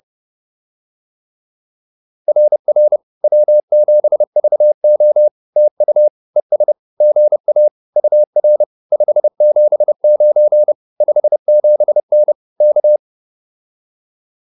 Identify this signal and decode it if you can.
Morse code, heard as a keyed tone, about 30 wpm, 600 Hz. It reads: RR W7VO TU ES GA UR 579 57N K